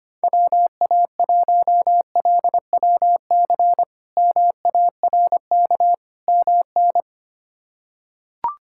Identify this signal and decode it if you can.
Morse code, heard as a keyed tone, about 25 wpm, 700 Hz. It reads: WA1LWC MARK MN
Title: K1USN SST QSO Practice - 25wpm